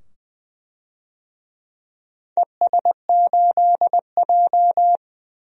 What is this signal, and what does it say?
Morse code, heard as a keyed tone, about 20 wpm, 700 Hz. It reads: ES8J
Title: DXCC-100 Call Signs - 4 Characters - 20wpm